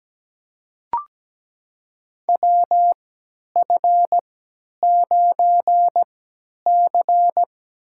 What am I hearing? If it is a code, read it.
Morse code, heard as a keyed tone, about 17 wpm, spaced as for 10 wpm, 700 Hz. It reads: WF9C
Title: USA Call Signs - 4 Characters - 3x Character Spacing - 17wpm